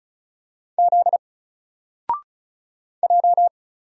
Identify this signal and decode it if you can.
Morse code, heard as a keyed tone, about 36 wpm, 700 Hz. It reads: Z J
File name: Single-Letter-Number-35wpm